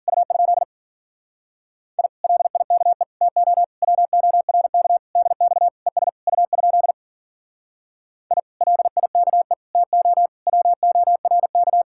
Morse code, heard as a keyed tone, about 52 wpm, 700 Hz. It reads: U? I LIKE TO WORK DX ES U? I LIKE TO WORK